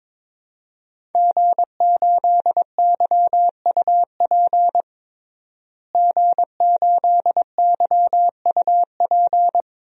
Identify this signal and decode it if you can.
Morse code, heard as a keyed tone, about 21 wpm, 700 Hz. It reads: G8YUP G8YUP
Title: DXCC-100 Call Signs - 5 Characters - Repeated 2x - 22wpm